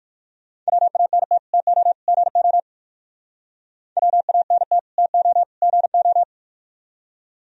Morse code, heard as a keyed tone, about 58 wpm, 700 Hz. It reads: WANT TO GO WANT TO GO